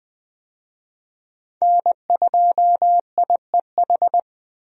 Morse code, heard as a keyed tone, about 20 wpm, 700 Hz. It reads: N2IEH
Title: USA Call Signs - 5 Characters - Repeated 3x - 20wpm